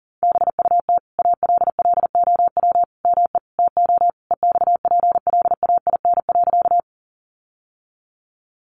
Morse code, heard as a keyed tone, about 37 wpm, 700 Hz. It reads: BUT ALLOW ME TO EXPLAIN.
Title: Sentences from Top 1100 Words 40wpm